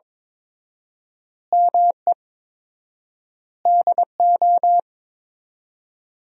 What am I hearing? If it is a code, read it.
Morse code, heard as a keyed tone, about 22 wpm, 700 Hz. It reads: ME DO
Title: 4x Word-Spacing - Sentences from Top 300 Words - 22wpm